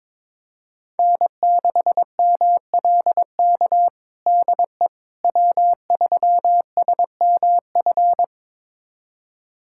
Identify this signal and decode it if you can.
Morse code, heard as a keyed tone, about 23 wpm, 700 Hz. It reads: N6MLK DE W3SMF
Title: USA Callsign DE Callsign - 5 Characters - 22wpm